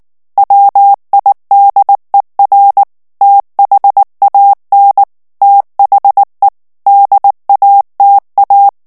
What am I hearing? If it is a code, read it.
Morse code, heard as a keyed tone, about 19 wpm, 800 Hz. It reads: WIDER THAN THE DATA
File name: ANGLAIS texte clair 3 vitesse 18 mots minute